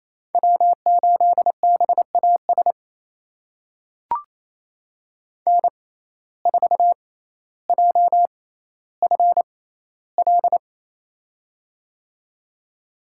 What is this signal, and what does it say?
Morse code, heard as a keyed tone, about 28 wpm, spaced as for 14 wpm, 700 Hz. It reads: W8BAS N4JFL
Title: USA Call Signs - 5 Characters - 6x Character Spacing - 28wpm